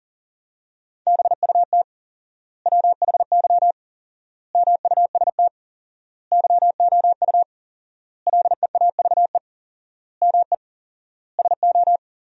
Morse code, heard as a keyed tone, about 40 wpm, 700 Hz. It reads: BUT WHY MUST YOU LEAVE ME SO
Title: 4x Word-Spacing - Sentences from Top 400 Words - 40wpm